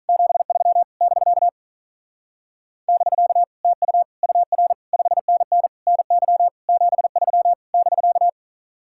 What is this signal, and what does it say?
Morse code, heard as a keyed tone, about 45 wpm, 700 Hz. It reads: 73 <BK> <BK> TU UR 5NN NY 73 <BK>